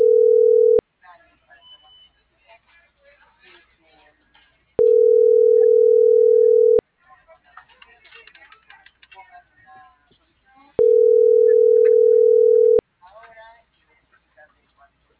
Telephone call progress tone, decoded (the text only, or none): ringback tone